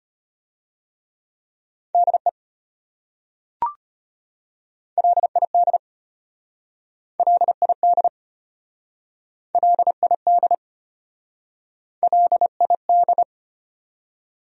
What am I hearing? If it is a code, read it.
Morse code, heard as a keyed tone, about 34 wpm, 700 Hz. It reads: DE LID LID LID LID